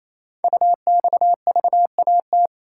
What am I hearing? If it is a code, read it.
Morse code, heard as a keyed tone, about 29 wpm, 700 Hz. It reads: UXVAT
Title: Code Groups - 5 Letters - 28wpm